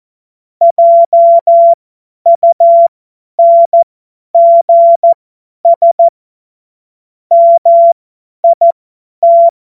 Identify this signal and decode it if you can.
Morse code, heard as a keyed tone, about 14 wpm, spaced as for 10 wpm, 680 Hz. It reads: JUNGS MIT
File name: Morsekurs / Witze / Witz 30. November 2021, 50 F70